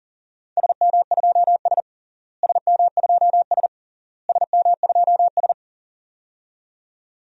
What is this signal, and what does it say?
Morse code, heard as a keyed tone, about 40 wpm, 700 Hz. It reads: SM2S SM2S SM2S
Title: DXCC-100 Call Signs - 4 Characters - Repeated 3x - 40wpm